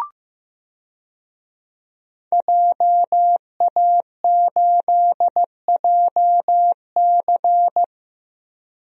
Morse code, heard as a keyed tone, about 15 wpm, 700 Hz. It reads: JA8JC
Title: DXCC-100 Call Signs - 5 Characters - Repeated 2x - 15wpm